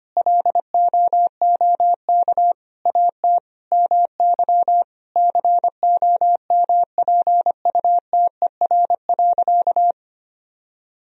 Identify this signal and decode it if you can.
Morse code, heard as a keyed tone, about 25 wpm, 700 Hz. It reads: LOOK AT MY COMPUTER.
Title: Sentences from Top 1000 Words - Review Part 3 - 25wpm